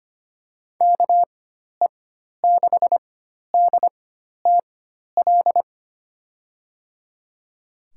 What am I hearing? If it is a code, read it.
Morse code, heard as a keyed tone, about 25 wpm, spaced as for 12 wpm, 700 Hz. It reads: KE6DTL
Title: USA Call Signs - 6 Characters - 4x Character Spacing - 25wpm